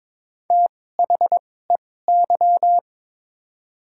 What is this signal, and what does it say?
Morse code, heard as a keyed tone, about 22 wpm, spaced as for 16 wpm, 700 Hz. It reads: THEY
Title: 2x Character Spacing - Top 100 Words - 22wpm